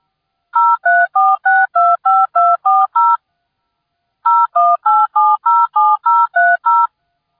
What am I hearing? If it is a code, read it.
DTMF, heard as telephone touch tones, tones of about 220 ms. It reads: 034625240 018707030